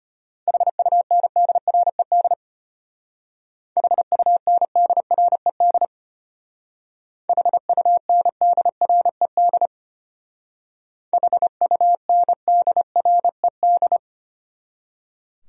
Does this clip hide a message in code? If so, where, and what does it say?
Yes: Morse code from 0.5 to 14.0 s, heard as a keyed tone, about 33 wpm, 700 Hz. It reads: HUNDRED HUNDRED HUNDRED HUNDRED